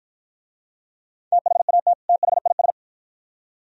Morse code, heard as a keyed tone, about 48 wpm, 700 Hz. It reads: THAT THIS